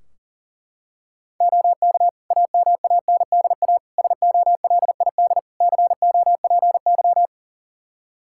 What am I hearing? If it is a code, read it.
Morse code, heard as a keyed tone, about 40 wpm, 700 Hz. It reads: OK AMANDA SOLID COPY
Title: K1USN SST QSO Practice 40wpm